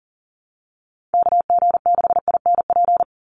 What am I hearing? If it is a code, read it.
Morse code, heard as a keyed tone, about 40 wpm, 700 Hz. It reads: KG6INP
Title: USA Call Signs 40wpm